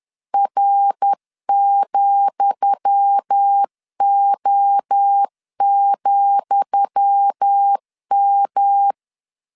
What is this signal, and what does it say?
Morse code, heard as a keyed tone, about 10 wpm, 800 Hz. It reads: R,O,M
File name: LEZIONI / 10 SIGNS / 10D COMA 10wpm 3013